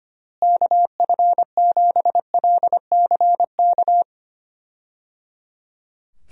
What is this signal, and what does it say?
Morse code, heard as a keyed tone, about 25 wpm, 700 Hz. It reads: KF7LCK
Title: USA Call Signs - 6 Characters - Repeated 3x - 25wpm